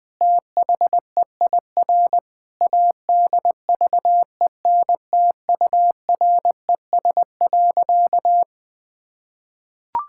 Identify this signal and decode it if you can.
Morse code, heard as a keyed tone, about 21 wpm, 700 Hz. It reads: THEIR ADVENTURES.
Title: Sentences with 2-Word Phrases - 20wpm